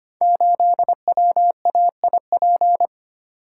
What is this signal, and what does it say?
Morse code, heard as a keyed tone, about 25 wpm, 700 Hz. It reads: 8WAIP
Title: Code Groups - 5 Letter:Numbers - 25wpm